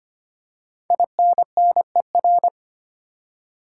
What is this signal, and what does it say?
Morse code, heard as a keyed tone, about 25 wpm, 700 Hz. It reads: INNER
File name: 5 Letter Words - First 3 Letters Given - 25wpm